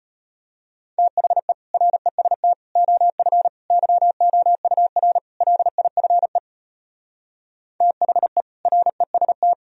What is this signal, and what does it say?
Morse code, heard as a keyed tone, about 33 wpm, 700 Hz. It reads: THE REST OF YOUR LIFE THE REST